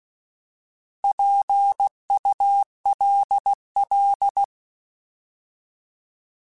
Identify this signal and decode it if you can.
Morse code, heard as a keyed tone, about 16 wpm, 800 Hz. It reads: PULL
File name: Alphabetical / Alphabetical mp3s / Ps @15